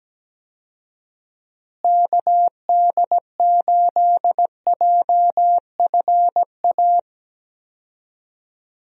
Morse code, heard as a keyed tone, about 17 wpm, 700 Hz. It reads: KD8JFA
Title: USA Call Signs - 6 Characters - Repeated 3x - 17wpm